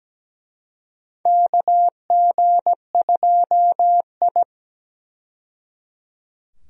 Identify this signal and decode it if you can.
Morse code, heard as a keyed tone, about 17 wpm, 700 Hz. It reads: KG2I
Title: USA Call Signs - 4 Characters - 17wpm